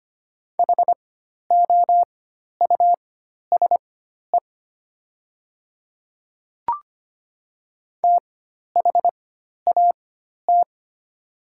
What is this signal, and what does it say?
Morse code, heard as a keyed tone, about 25 wpm, spaced as for 12 wpm, 700 Hz. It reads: HOUSE THAT
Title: 4x Character Spacing - Top 500 Words - 25wpm